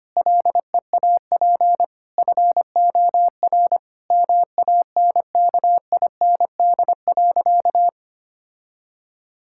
Morse code, heard as a keyed tone, about 25 wpm, 700 Hz. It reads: LEAP FOR MANKIND.
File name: practice / 25wpm / Sentences with 9-Word Phrases - 25wpm